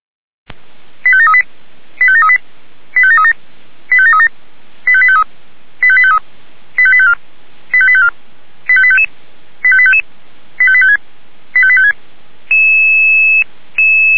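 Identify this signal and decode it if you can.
Five-tone selective call (ZVEI), heard as quick runs of five tones, tones of about 70 ms. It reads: 85738 85738 85737 85737 85853 85853 85854 85854 85855 85855 85856 85856
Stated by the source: ZVEI five-tone selective call received on SDR.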